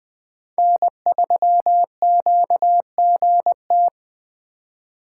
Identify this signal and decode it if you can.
Morse code, heard as a keyed tone, about 20 wpm, 700 Hz. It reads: N3QGT